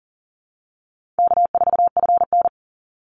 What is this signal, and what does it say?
Morse code, heard as a keyed tone, about 39 wpm, 700 Hz. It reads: K4FN
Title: Easy Call Signs Repeated 2x - 40wpm